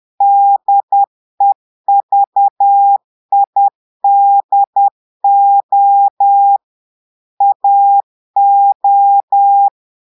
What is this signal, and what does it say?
Morse code, heard as a keyed tone, about 10 wpm, 800 Hz. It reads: DEVIDO AO
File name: Portugais - texte en clair 1 - vitesse - 10 mots minute